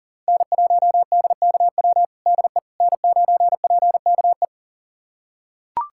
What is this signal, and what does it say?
Morse code, heard as a keyed tone, about 37 wpm, 700 Hz. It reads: N1DKW DE N9PKE
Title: USA Callsign DE Callsign - 5 Characters - 40wpm